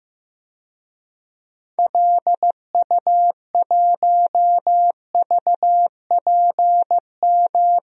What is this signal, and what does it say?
Morse code, heard as a keyed tone, about 15 wpm, 700 Hz. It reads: LU1VPM
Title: DXCC-100 Call Signs - 6 Characters - Repeated 2x - 15wpm